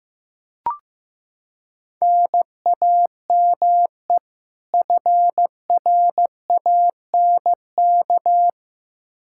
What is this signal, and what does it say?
Morse code, heard as a keyed tone, about 15 wpm, 700 Hz. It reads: NAME FRANK
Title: K1USN SST QSO Practice 15wpm